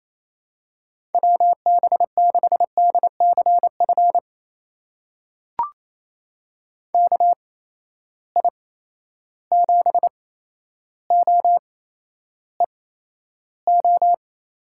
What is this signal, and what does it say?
Morse code, heard as a keyed tone, about 28 wpm, spaced as for 12 wpm, 700 Hz. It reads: WB6DCF KI7OEO
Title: USA Call Signs - 6 Characters - 8x Character Spacing - 28wpm